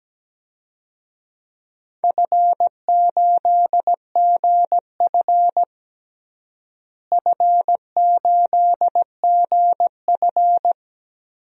Morse code, heard as a keyed tone, about 17 wpm, 700 Hz. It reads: F8GF F8GF
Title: DXCC-100 Call Signs - 4 Characters - Repeated 3x - 17wpm